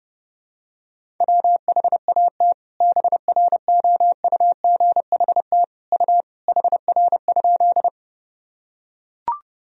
Morse code, heard as a keyed tone, about 32 wpm, 700 Hz. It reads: WHAT BROUGHT U HR?
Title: Sentences from Top 500 Words and Abbreviations 30wpm